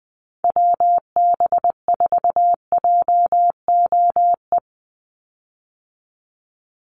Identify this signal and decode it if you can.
Morse code, heard as a keyed tone, about 20 wpm, 700 Hz. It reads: WB4JOE
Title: USA Call Signs 20wpm